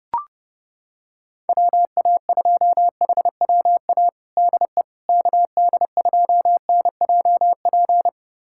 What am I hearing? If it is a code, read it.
Morse code, heard as a keyed tone, about 30 wpm, 700 Hz. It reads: WA2HWA DE KD2NJP
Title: USA Callsign DE Callsign - 6 Characters - 30wpm